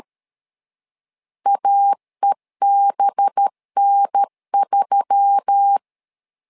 Morse code, heard as a keyed tone, about 13 wpm, 800 Hz. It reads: AEBN3
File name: LEZIONI / 5 LBVC / 5 LBVC 12wpm 1955